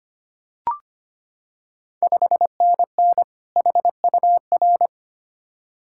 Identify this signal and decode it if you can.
Morse code, heard as a keyed tone, about 25 wpm, 700 Hz. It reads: 5NN HUR
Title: Ontario QSO Party Exchange 25wpm